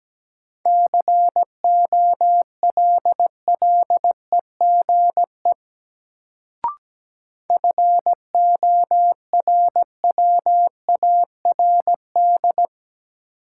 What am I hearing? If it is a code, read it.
Morse code, heard as a keyed tone, about 17 wpm, 700 Hz. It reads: COLLEGE FORWARD
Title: Sentences from Top 1300 Words - New Word Drill - 17wpm